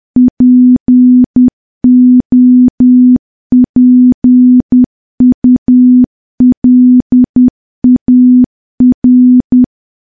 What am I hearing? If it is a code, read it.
Morse code, heard as a keyed tone, about 10 wpm, 250 Hz. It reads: POPULAR
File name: morse / morse-training-10WPM